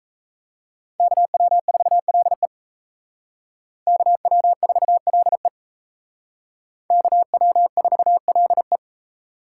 Morse code, heard as a keyed tone, about 35 wpm, 700 Hz. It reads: KW4LE KW4LE KW4LE
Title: Speed Racing - USA 2x2 Call Signs 28wpm